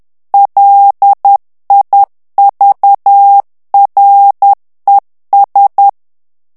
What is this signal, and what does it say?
Morse code, heard as a keyed tone, about 10 wpm, 800 Hz. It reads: LIVRES
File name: FRANCAIS TEXTE CLAIR 2 VITESSE 10 MOTS MINUTES